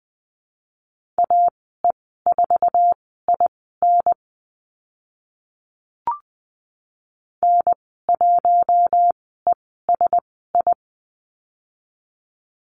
Morse code, heard as a keyed tone, about 20 wpm, spaced as for 14 wpm, 700 Hz. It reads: AE4IN N1ESI
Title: 048 Lesson 9g - Number 4 - Call Signs 20wpm